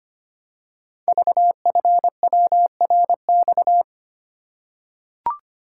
Morse code, heard as a keyed tone, about 25 wpm, 700 Hz. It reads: VFWRX
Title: Code Groups - 5 Letters - 25wpm